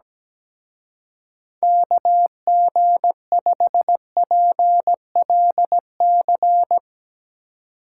Morse code, heard as a keyed tone, about 19 wpm, 700 Hz. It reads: KG5PLC
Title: Speed Racing - USA 2x3 Call Signs 17wpm